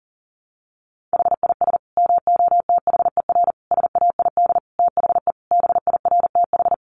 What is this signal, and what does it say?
Morse code, heard as a keyed tone, about 40 wpm, 700 Hz. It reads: HIS MOTHER SAID THE BIRTH
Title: Sentences from Top 800 Words 40wpm